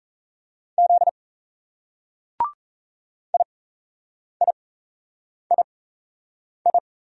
Morse code, heard as a keyed tone, about 36 wpm, spaced as for 8 wpm, 700 Hz. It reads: Z IIII